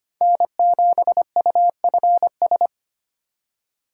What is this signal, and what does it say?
Morse code, heard as a keyed tone, about 25 wpm, 700 Hz. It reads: N7UFS
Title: USA Call Signs - 5 Characters - 4x Character Spacing - 25wpm